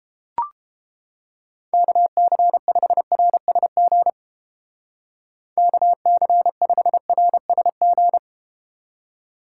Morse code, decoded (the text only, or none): KC5RSG KC5RSG